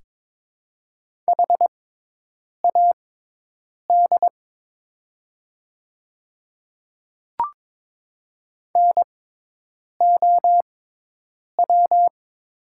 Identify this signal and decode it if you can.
Morse code, heard as a keyed tone, about 22 wpm, spaced as for 8 wpm, 700 Hz. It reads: HAD NOW
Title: 6x Character Spacing - Top 100 Words - 22wpm